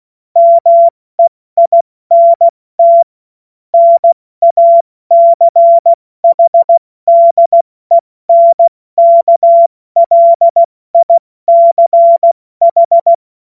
Morse code, heard as a keyed tone, about 16 wpm, spaced as for 14 wpm, 680 Hz. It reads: MEINT NACHDENKLICH
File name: Morsekurs / Witze / Witz 02. Mai 2023 70 F80